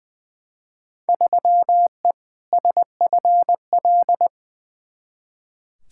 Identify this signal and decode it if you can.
Morse code, heard as a keyed tone, about 20 wpm, 700 Hz. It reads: 3E SFL
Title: Field Day Exchanges Encore - 20wpm